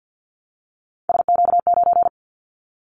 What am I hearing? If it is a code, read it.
Morse code, heard as a keyed tone, about 50 wpm, 700 Hz. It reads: SQ9
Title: Callsign Prefixes for Top 25 DXCC - 50wpm